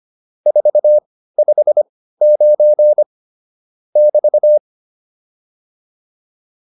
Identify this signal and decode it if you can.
Morse code, heard as a keyed tone, about 25 wpm, spaced as for 15 wpm, 600 Hz. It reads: 459 <BT>
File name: QSO307 15